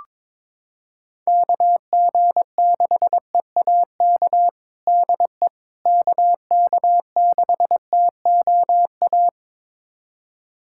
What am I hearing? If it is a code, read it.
Morse code, heard as a keyed tone, about 22 wpm, 700 Hz. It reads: KG6EAK DE KK6TOA
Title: USA Callsign DE Callsign - 6 Characters - 22wpm